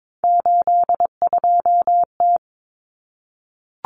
Morse code, heard as a keyed tone, about 22 wpm, 700 Hz. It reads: 82T
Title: Common Cut Numbers - 3-Digit 22wpm